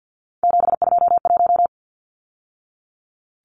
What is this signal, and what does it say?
Morse code, heard as a keyed tone, about 50 wpm, 700 Hz. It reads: 721
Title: Numbers - 3-Digit 50wpm